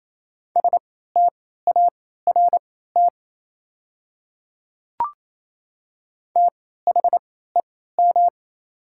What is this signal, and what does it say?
Morse code, heard as a keyed tone, about 28 wpm, spaced as for 16 wpm, 690 Hz. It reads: START THEM
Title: 3x Character Spacing - Top 300 Words - 28wpm